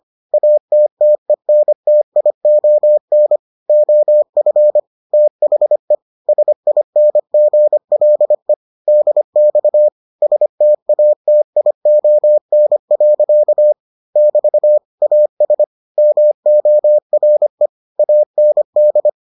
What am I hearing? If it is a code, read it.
Morse code, heard as a keyed tone, about 25 wpm, 600 Hz. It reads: ATTENTION OF THE SINGLE DX STATION. <BT> AS MORE AND